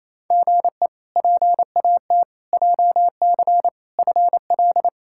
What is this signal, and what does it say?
Morse code, heard as a keyed tone, about 28 wpm, 700 Hz. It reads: GE PAT JC FL
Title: K1USN SST QSO Practice - 28wpm